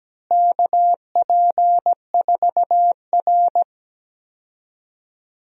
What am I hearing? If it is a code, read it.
Morse code, heard as a keyed tone, about 17 wpm, 700 Hz. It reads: KP4R
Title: DXCC-100 Call Signs - 4 Characters - Repeated 3x - 17wpm